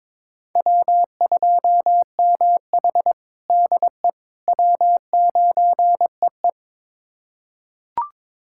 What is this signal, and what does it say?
Morse code, heard as a keyed tone, about 22 wpm, 700 Hz. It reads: W2MH DE W9EE
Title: USA Callsign DE Callsign - 4 Characters - 22wpm